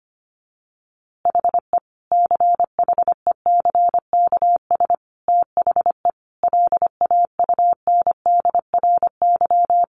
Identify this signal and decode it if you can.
Morse code, heard as a keyed tone, about 25 wpm, 700 Hz. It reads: HE CHECKS THE LAUNDRY
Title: Easy Phrases 25wpm